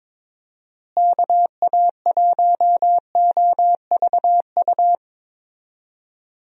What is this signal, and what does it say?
Morse code, heard as a keyed tone, about 22 wpm, 700 Hz. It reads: KA1OVU
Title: USA Call Signs - 6 Characters - Repeated 3x - 22wpm